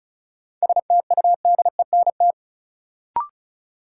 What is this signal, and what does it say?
Morse code, heard as a keyed tone, about 36 wpm, 700 Hz. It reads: STUDENT
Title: Sentences from Top 1000 Words - New Word Drill - 35wpm